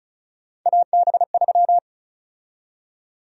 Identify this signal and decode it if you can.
Morse code, heard as a keyed tone, about 36 wpm, 700 Hz. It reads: AB3
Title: USA Call Sign Prefixes - Repeated 2x - 35wpm